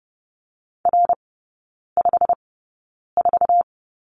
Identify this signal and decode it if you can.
Morse code, heard as a keyed tone, about 30 wpm, spaced as for 9 wpm, 700 Hz. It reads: R54
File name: course / 30:10wpm / 087 Lesson 15e - Number 5 - Three Characters 30wpm